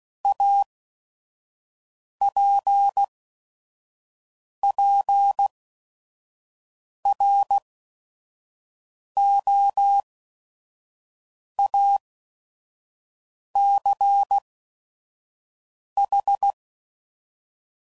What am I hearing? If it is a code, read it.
Morse code, heard as a keyed tone, about 16 wpm, spaced as for 5 wpm, 800 Hz. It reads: APPROACH
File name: CW Code Groups @5WPM / Technical Subjects / Gates Intro